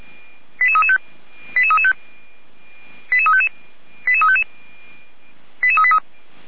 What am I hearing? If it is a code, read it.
Five-tone selective call (ZVEI), heard as quick runs of five tones, tones of about 70 ms. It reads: 88335 88335 88355 88355 88383